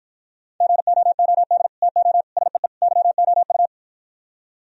Morse code, heard as a keyed tone, about 52 wpm, 700 Hz. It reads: GOOD TO SEE YOU